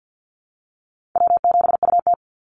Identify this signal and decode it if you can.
Morse code, heard as a keyed tone, about 50 wpm, 700 Hz. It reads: W7UT
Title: Call Signs Repeated 3x - 50wpm